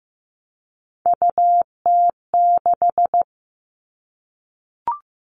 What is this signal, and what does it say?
Morse code, heard as a keyed tone, about 15 wpm, 700 Hz. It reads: UT6
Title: All Cut Numbers - 3-Digit 15wpm